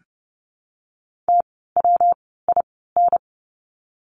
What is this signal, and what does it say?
Morse code, heard as a keyed tone, about 30 wpm, 700 Hz. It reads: T W I N
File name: course / 30:10wpm / 115 Lesson 19f - Letter w - Words 30wpm